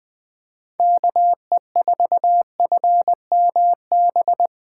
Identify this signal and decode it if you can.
Morse code, heard as a keyed tone, about 20 wpm, 700 Hz. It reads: KE4FMB
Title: Speed Racing - USA 2x3 Call Signs 17wpm